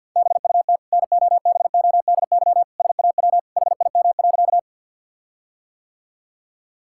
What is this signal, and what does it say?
Morse code, heard as a keyed tone, about 50 wpm, 700 Hz. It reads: BUT NOBODY SAW HIM.
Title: Sentences from Top 1600 Words 50wpm